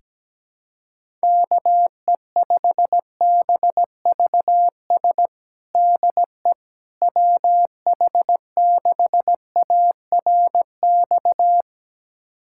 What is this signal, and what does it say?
Morse code, heard as a keyed tone, about 17 wpm, 700 Hz. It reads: KE5BVS DE WH6ARX
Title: USA Callsign DE Callsign - 6 Characters - 17wpm